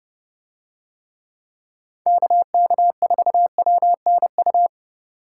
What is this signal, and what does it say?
Morse code, heard as a keyed tone, about 29 wpm, 700 Hz. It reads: KK4WNU